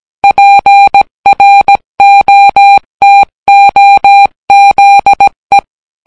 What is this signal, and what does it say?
Morse code, heard as a keyed tone, about 18 wpm, 800 Hz. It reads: PROTOZE